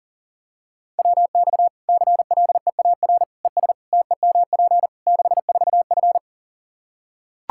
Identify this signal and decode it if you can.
Morse code, heard as a keyed tone, about 40 wpm, 700 Hz. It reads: WX CLEAR ES TEMP 64F
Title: K1USN SST QSO Practice 40wpm